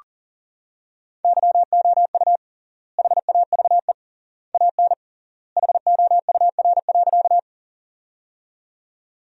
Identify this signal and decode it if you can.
Morse code, heard as a keyed tone, about 40 wpm, 700 Hz. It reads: YOU HAVE AN HOUR.
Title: 3x Word-Spacing - Sentences from Top 400 Words - 40wpm